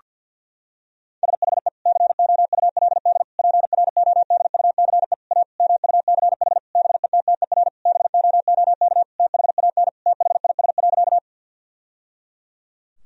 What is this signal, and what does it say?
Morse code, heard as a keyed tone, about 50 wpm, 700 Hz. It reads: SHE COULD PRODUCE A MUCH BETTER BOOK THAN THIS.
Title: Sentences from Top 1000 Words - Review Part 3 - 50wpm